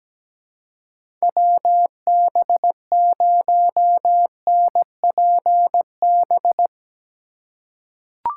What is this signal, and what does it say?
Morse code, heard as a keyed tone, about 17 wpm, 700 Hz. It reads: WB0NPB
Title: USA Call Signs - 4x Character Spacing - 17wpm